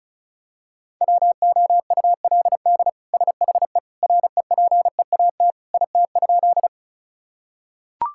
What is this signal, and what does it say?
Morse code, heard as a keyed tone, about 34 wpm, 700 Hz. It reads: WOULD SHE REPEAT IT?
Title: Sentences from Top 1600 Words 35wpm